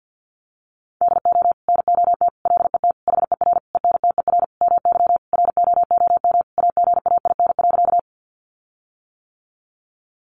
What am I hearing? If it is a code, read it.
Morse code, heard as a keyed tone, about 50 wpm, 700 Hz. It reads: DO NOT LET HER ENTER MY ROOM AGAIN.
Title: Sentences from Top 1300 Words 50wpm